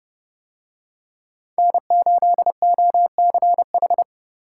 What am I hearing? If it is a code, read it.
Morse code, heard as a keyed tone, about 28 wpm, 700 Hz. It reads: N8OCH